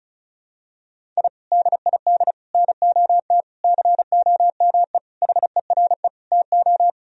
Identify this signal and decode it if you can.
Morse code, heard as a keyed tone, about 34 wpm, 700 Hz. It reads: I DID NOT COME HERE TO
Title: Sentences from Top 1000 Words - Review Part 1 - 35wpm